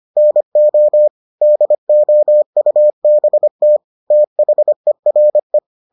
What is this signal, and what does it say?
Morse code, heard as a keyed tone, about 25 wpm, 600 Hz. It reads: NO DOUBT THERE